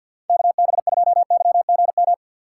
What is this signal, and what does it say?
Morse code, heard as a keyed tone, about 48 wpm, 700 Hz. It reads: KB2YGM
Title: USA Call Signs - 3x Character Spacing - 50wpm